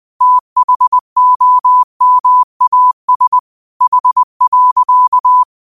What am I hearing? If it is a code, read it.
Morse code, heard as a keyed tone, about 20 wpm, 1000 Hz. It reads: THOMAS H.